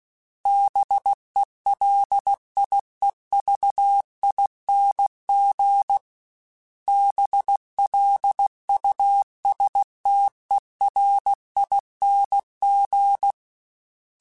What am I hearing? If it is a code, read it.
Morse code, heard as a keyed tone, about 16 wpm, 800 Hz. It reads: BELIEVING BLUSTERING
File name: Special INGs / Special INGs-mp3s / Sngl ing @15